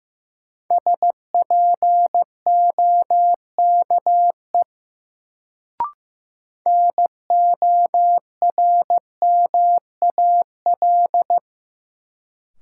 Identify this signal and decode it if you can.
Morse code, heard as a keyed tone, about 15 wpm, 700 Hz. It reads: SPOKE NORMAL
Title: Sentences from Top 1200 Words - New Word Drill - 15wpm